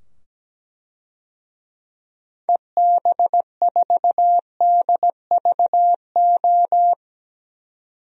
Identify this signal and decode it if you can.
Morse code, heard as a keyed tone, about 17 wpm, 700 Hz. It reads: EB4DVO
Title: DXCC-100 Call Signs - 6 Characters - 17wpm